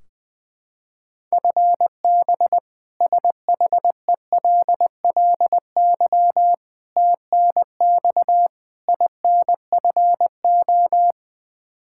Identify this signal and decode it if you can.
Morse code, heard as a keyed tone, about 21 wpm, 700 Hz. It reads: FB SHELLY TNX INFO
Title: K1USN SST QSO Practice 20wpm